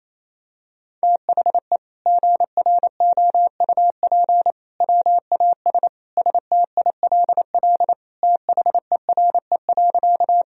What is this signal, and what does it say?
Morse code, heard as a keyed tone, about 27 wpm, 700 Hz. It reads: THE GROUP WAS STILL THERE.
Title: Sentences from Top 1000 Words - Review Part 1 - 28wpm